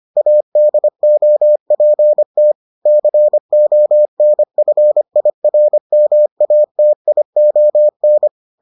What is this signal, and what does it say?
Morse code, heard as a keyed tone, about 25 wpm, 600 Hz. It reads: ADOPT CONFIRMATION